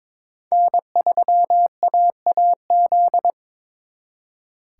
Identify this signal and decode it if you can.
Morse code, heard as a keyed tone, about 25 wpm, 700 Hz. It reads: N3AAZ
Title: Speed Racing - USA 1x3 Call Signs 22wpm